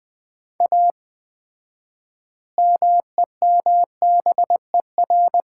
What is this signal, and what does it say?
Morse code, heard as a keyed tone, about 20 wpm, 700 Hz. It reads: A MEMBER
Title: 4x Word-Spacing - Sets of 3 Words - 20wpm